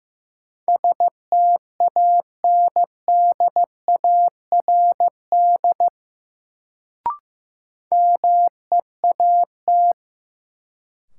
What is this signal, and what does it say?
Morse code, heard as a keyed tone, about 15 wpm, 700 Hz. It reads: STANDARD MEAT
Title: Sentences from Top 1200 Words - New Word Drill Alt - 15wpm